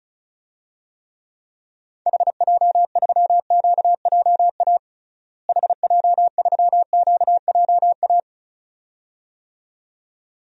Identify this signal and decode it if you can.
Morse code, heard as a keyed tone, about 36 wpm, 700 Hz. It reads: HJ3QJA HJ3QJA
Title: DXCC-100 Call Signs - 6 Characters - Repeated 2x - 35wpm